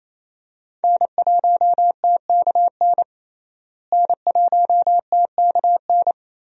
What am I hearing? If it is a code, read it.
Morse code, heard as a keyed tone, about 28 wpm, 700 Hz. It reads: N1TKN N1TKN
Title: USA Call Signs - 5 Characters - Repeated 2x - 28wpm